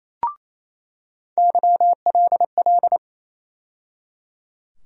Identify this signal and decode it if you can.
Morse code, heard as a keyed tone, about 28 wpm, 700 Hz. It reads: YLL
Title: USA Call Sign Postfixes - 3 Letters 28wpm